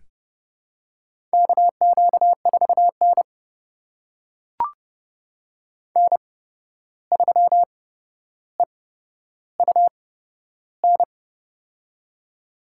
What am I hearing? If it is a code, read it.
Morse code, heard as a keyed tone, about 30 wpm, spaced as for 12 wpm, 700 Hz. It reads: KQ4N N3EUN
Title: USA Call Signs - 8x Character Spacing - 30wpm